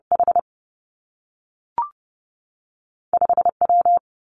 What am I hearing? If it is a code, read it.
Morse code, heard as a keyed tone, about 30 wpm, 700 Hz. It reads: H 5 W
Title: Two Challenging Letter-Number Combinations 30wpm